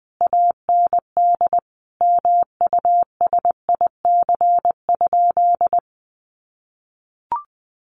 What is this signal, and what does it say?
Morse code, heard as a keyed tone, about 20 wpm, 700 Hz. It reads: AND MUSIC?
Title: Sentences from Top 1200 Words 20wpm